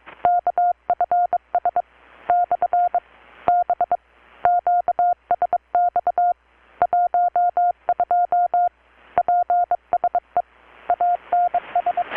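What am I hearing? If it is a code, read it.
Morse code, heard as a keyed tone, about 25 wpm, 720 Hz. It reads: KFS / B QSX 12 PSE ANS
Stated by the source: Morse from KPH's Night of Nights XX maritime radiotelegraphy event, received in Chiapas, Mexico about 3,800 km away on a Kenwood R-600 with a grounded 10 m long-wire antenna.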